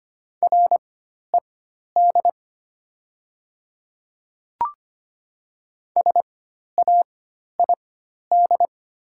Morse code, heard as a keyed tone, about 25 wpm, spaced as for 12 wpm, 700 Hz. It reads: RED SAID
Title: 4x Character Spacing - Top 500 Words - 25wpm